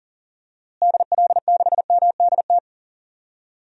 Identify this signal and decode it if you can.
Morse code, heard as a keyed tone, about 40 wpm, 700 Hz. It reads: DL6MDT
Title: DXCC-100 Call Signs - 6 Characters - Repeated 2x - 40wpm